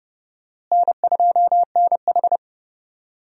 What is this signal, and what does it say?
Morse code, heard as a keyed tone, about 30 wpm, 700 Hz. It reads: N2NH
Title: USA Call Signs - 4 Characters - Repeated 2x - 30wpm